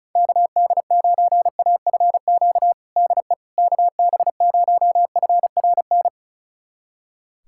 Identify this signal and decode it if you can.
Morse code, heard as a keyed tone, about 34 wpm, 700 Hz. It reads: KD9AFQ DE KB0FRN
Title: USA Callsign DE Callsign - 6 Characters - 35wpm